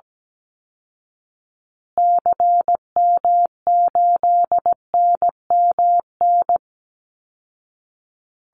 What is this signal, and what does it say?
Morse code, heard as a keyed tone, about 17 wpm, 700 Hz. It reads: CM8NMN
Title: Call Signs Repeated 2x - 17wpm